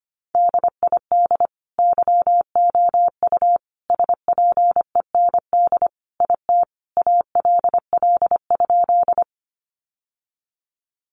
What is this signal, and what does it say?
Morse code, heard as a keyed tone, about 25 wpm, 700 Hz. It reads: DID YOU SPEND IT ALL?
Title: Sentences from Top 800 Words 25wpm